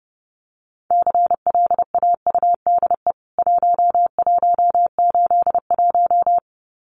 Morse code, heard as a keyed tone, about 28 wpm, 700 Hz. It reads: CLAUDE 1181
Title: CWT Exchange 30wpm